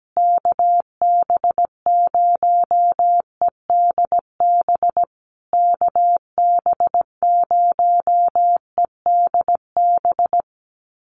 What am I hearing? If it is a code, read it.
Morse code, heard as a keyed tone, about 17 wpm, 700 Hz. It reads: KB0EDB KB0EDB
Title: USA Call Signs Repeated 3x 17wpm